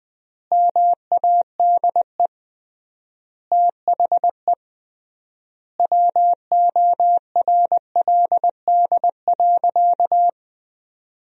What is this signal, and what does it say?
Morse code, heard as a keyed tone, about 20 wpm, 700 Hz. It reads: MADE THE WORLD.
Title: 3x Word-Spacing - Sentences from Top 200 Words - 20wpm